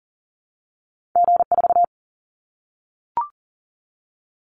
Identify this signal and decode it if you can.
Morse code, heard as a keyed tone, about 40 wpm, 700 Hz. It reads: G4
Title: Callsign Prefixes for Top 25 DXCC - 40wpm